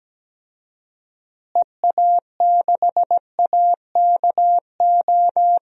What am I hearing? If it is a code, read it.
Morse code, heard as a keyed tone, about 17 wpm, 700 Hz. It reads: EA6AKO
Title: DXCC-100 Call Signs - 6 Characters - Repeated 3x - 17wpm